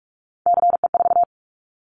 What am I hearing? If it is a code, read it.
Morse code, heard as a keyed tone, about 44 wpm, 700 Hz. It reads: CE4
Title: Callsign Prefixes for Top 100 DXCC - 45wpm